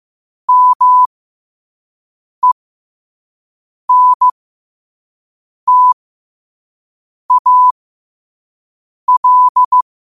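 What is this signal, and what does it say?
Morse code, heard as a keyed tone, about 15 wpm, spaced as for 5 wpm, 1000 Hz. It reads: MENTAL
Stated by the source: Morse code practice at 5 wpm.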